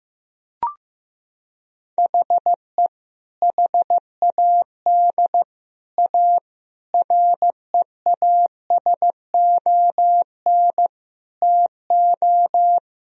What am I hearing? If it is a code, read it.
Morse code, heard as a keyed tone, about 15 wpm, 700 Hz. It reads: HE HAD A REASON TO
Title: Sentences from Top 1000 Words - Review Part 1 - 15wpm